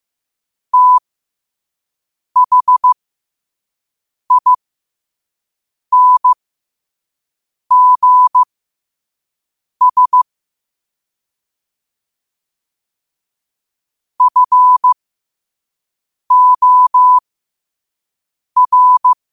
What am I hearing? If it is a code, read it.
Morse code, heard as a keyed tone, about 15 wpm, spaced as for 5 wpm, 1000 Hz. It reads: THINGS FOR
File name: qotdfiles / 5wpm